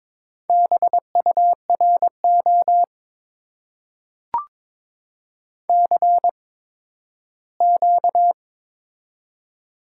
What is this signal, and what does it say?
Morse code, heard as a keyed tone, about 22 wpm, spaced as for 13 wpm, 700 Hz. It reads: BURO CQ
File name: Top 90 QSO Elements - 8x Character Spacing - 22wpm